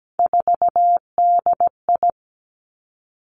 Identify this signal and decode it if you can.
Morse code, heard as a keyed tone, about 17 wpm, 700 Hz. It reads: 4DI
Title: Three Challenging Letter-Number Combinations 17wpm